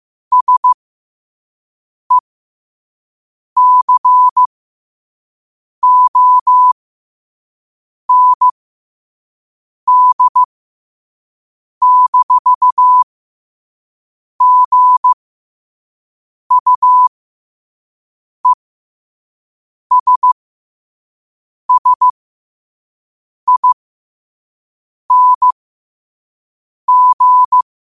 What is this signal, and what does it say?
Morse code, heard as a keyed tone, about 15 wpm, spaced as for 5 wpm, 1000 Hz. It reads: SECOND-GUESSING
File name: qotdfiles / 5wpm